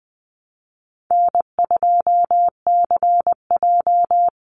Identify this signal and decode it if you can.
Morse code, heard as a keyed tone, about 20 wpm, 700 Hz. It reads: N2CJ
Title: Call Signs Repeated 3x - 20wpm